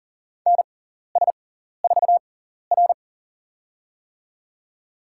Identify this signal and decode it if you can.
Morse code, heard as a keyed tone, about 40 wpm, spaced as for 14 wpm, 700 Hz. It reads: NS4R
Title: USA Call Signs - 4 Characters - 6x Character Spacing - 40wpm